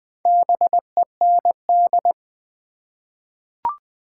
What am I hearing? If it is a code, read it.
Morse code, heard as a keyed tone, about 20 wpm, 700 Hz. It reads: BEND
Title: Four Letter Words - 20wpm